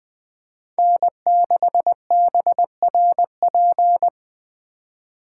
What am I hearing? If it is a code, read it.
Morse code, heard as a keyed tone, about 20 wpm, 700 Hz. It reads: N6BRP
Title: Speed Racing - USA 1x3 Call Signs 17wpm